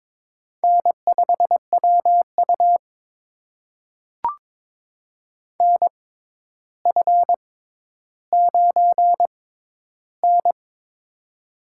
Morse code, heard as a keyed tone, about 22 wpm, spaced as for 11 wpm, 700 Hz. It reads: N5WU NF9N
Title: USA Call Signs - 4 Characters - 6x Character Spacing - 22wpm